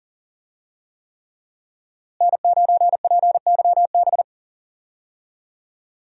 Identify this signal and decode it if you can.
Morse code, heard as a keyed tone, about 41 wpm, 700 Hz. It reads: N9PYB
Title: USA Call Signs - 5 Characters - Repeated 3x - 40wpm